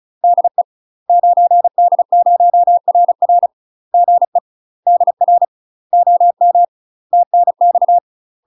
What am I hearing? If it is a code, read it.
Morse code, heard as a keyed tone, about 34 wpm, 700 Hz. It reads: DE 9D0RR GE DR OM TNX
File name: UK MASTER QSOs TRAINING / 35WPM / UK MASTER QSOs 0017